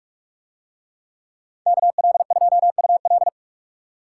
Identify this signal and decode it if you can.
Morse code, heard as a keyed tone, about 44 wpm, 700 Hz. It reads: KL2UL